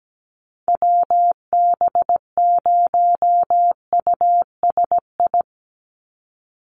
Morse code, heard as a keyed tone, about 17 wpm, 700 Hz. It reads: WB0USI
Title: NA SOTA Activator Callsigns 17wpm